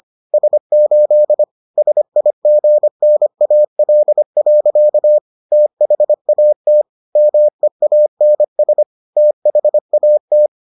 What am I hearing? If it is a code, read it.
Morse code, heard as a keyed tone, about 25 wpm, 600 Hz. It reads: S8 SIGNAL. THAT MEANS THAT